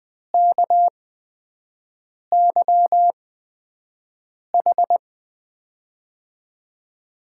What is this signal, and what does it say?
Morse code, heard as a keyed tone, about 20 wpm, spaced as for 5 wpm, 700 Hz. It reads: KYH
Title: USA Call Signs - 6 Characters - 8x Character Spacing - 20wpm